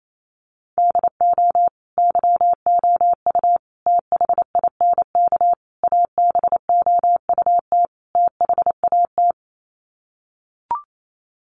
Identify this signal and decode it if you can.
Morse code, heard as a keyed tone, about 28 wpm, 700 Hz. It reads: DO YOU THINK ABOUT THAT
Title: Sets of 5 Words Encore2 28wpm